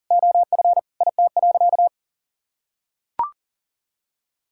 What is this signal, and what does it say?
Morse code, heard as a keyed tone, about 40 wpm, 700 Hz. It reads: OF IT.
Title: Sentences from Top 1000 Words - Review Part 6 - 40wpm